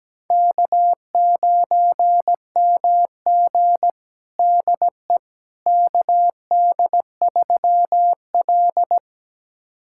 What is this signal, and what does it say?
Morse code, heard as a keyed tone, about 17 wpm, 700 Hz. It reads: K9MG DE KD3L
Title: USA Callsign DE Callsign - 4 Characters - 17wpm